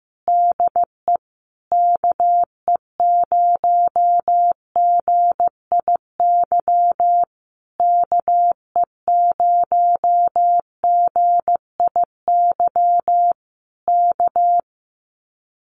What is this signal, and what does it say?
Morse code, heard as a keyed tone, about 15 wpm, 700 Hz. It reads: DE KE0GIY KE0GIY K
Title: Common QSO Phrases Encore 15wpm